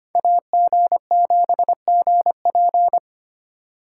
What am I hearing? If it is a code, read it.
Morse code, heard as a keyed tone, about 33 wpm, 700 Hz. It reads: AG7GP